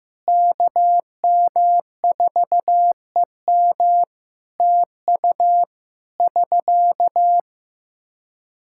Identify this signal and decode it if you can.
Morse code, heard as a keyed tone, about 15 wpm, 700 Hz. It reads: KM4EM TU <SK>
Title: Practice QSO - Exchange 4 - Ending - 15wpm